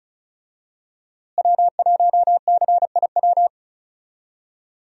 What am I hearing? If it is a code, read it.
Morse code, heard as a keyed tone, about 34 wpm, 700 Hz. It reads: W1CIW
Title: USA Call Signs - 5 Characters - Repeated 2x - 35wpm